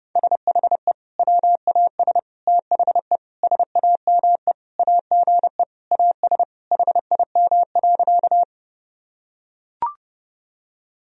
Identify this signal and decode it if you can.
Morse code, heard as a keyed tone, about 30 wpm, 700 Hz. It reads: SHE WAS THE SAME AGE AS HIM.
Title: Sentences from Top 1000 Words - Review Part 3 - 30wpm